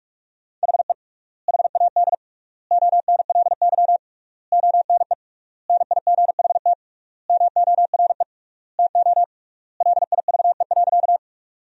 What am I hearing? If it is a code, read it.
Morse code, heard as a keyed tone, about 46 wpm, 700 Hz. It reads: HE HAD ONLY ONE NIGHT MORE TO LIVE.
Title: 3x Word-Spacing - Sentences from Top 300 Words - 45wpm